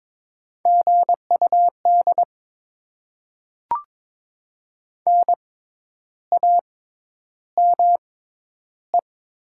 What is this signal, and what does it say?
Morse code, heard as a keyed tone, about 22 wpm, spaced as for 10 wpm, 700 Hz. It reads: GUD NAME
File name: Top 20 QSO Elements - 6x Character Spacing - 22wpm